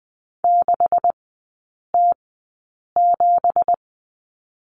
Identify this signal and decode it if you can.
Morse code, heard as a keyed tone, about 20 wpm, spaced as for 8 wpm, 700 Hz. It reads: 6T7
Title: 166 Lesson 27e - Number 7 - Three Characters 20wpm